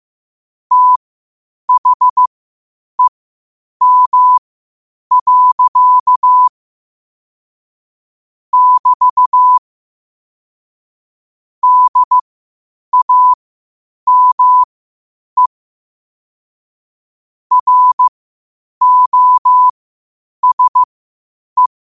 Morse code, heard as a keyed tone, about 15 wpm, spaced as for 8 wpm, 1000 Hz. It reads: THEM. <BT> DAME ROSE
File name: qotdfiles / 8wpm